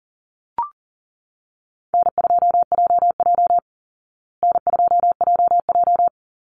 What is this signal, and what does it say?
Morse code, heard as a keyed tone, about 40 wpm, 700 Hz. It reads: N2JJ N2JJ
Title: Easy Call Signs Repeated 2x - 40wpm